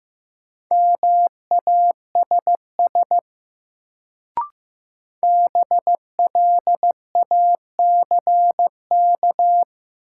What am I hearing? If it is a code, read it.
Morse code, heard as a keyed tone, about 15 wpm, 700 Hz. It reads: MASS BLACK
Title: Sets of 5 Words Encore - New Word Drill - 15wpm